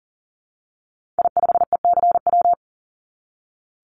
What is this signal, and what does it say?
Morse code, heard as a keyed tone, about 40 wpm, 700 Hz. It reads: I5ECW
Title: Call Signs - 40wpm